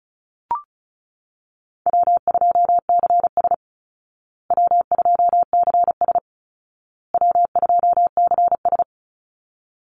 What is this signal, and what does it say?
Morse code, heard as a keyed tone, about 34 wpm, 700 Hz. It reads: W2CS W2CS W2CS
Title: Easy Call Signs Repeated 3x - 35wpm